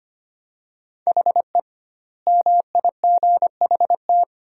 Morse code, heard as a keyed tone, about 25 wpm, 700 Hz. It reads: HE MIGHT
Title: 2x Word-Spacing - Sentences from Top 300 Words - 25wpm